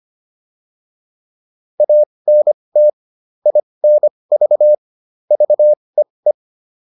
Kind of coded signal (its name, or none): Morse code